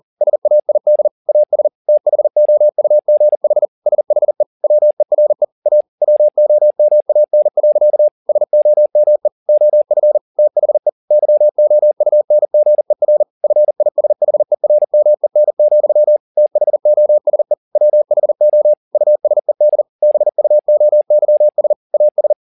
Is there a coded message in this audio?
Yes: Morse code from 0.2 to 22.3 s, heard as a keyed tone, about 40 wpm, 600 Hz. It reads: SAID AS THOUGH SHE WERE A WOMAN. SOME OF THE YOUNGER FISHERMEN, THOSE WHO USED BUOYS AS